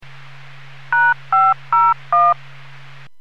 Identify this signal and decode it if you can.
DTMF, heard as telephone touch tones, tones of about 220 ms. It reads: #501